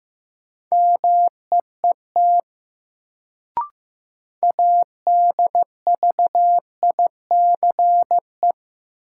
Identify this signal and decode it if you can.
Morse code, heard as a keyed tone, about 15 wpm, 700 Hz. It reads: MEET ADVICE
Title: Sentences from Top 1100 Words - New Word Drill Alt - 15wpm